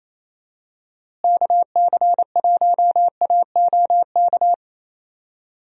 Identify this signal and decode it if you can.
Morse code, heard as a keyed tone, about 28 wpm, 700 Hz. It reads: KC1AOK
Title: USA Call Signs - 6 Characters - Repeated 2x - 28wpm